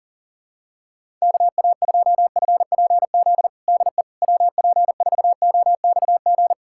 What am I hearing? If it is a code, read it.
Morse code, heard as a keyed tone, about 40 wpm, 700 Hz. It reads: KA2FPZ DE WP4OXG
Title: USA Callsign DE Callsign - 6 Characters - 40wpm